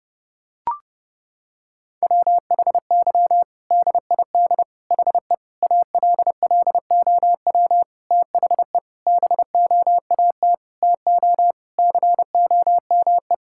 Morse code, heard as a keyed tone, about 30 wpm, 700 Hz. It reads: WHY DID HE ALLOW THE BOAT TO COME
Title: Sentences from Top 1000 Words - Review Part 5 - 30wpm